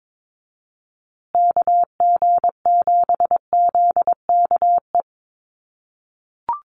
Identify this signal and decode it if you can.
Morse code, heard as a keyed tone, about 22 wpm, 700 Hz. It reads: KG7ZKE
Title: USA Call Signs Repeated 3x 22wpm